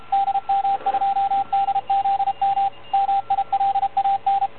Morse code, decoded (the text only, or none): DM2DZM MILAN